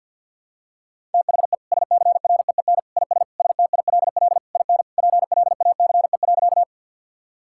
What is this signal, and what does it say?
Morse code, heard as a keyed tone, about 46 wpm, 700 Hz. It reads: THE SCREEN IS STILL IN PLACE.
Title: Sentences from Top 1000 Words - Review Part 6 - 50wpm